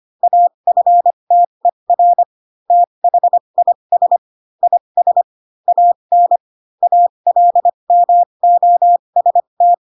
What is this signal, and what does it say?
Morse code, heard as a keyed tone, about 25 wpm, spaced as for 22 wpm, 700 Hz. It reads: AFTER THIS IS AN ALMOST
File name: Pinocchio / Pinocchio ch3 18 25 ik1hge